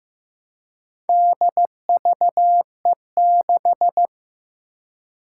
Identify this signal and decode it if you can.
Morse code, heard as a keyed tone, about 15 wpm, 700 Hz. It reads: DVE6
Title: Four Challenging Letter-Number Combinations 15wpm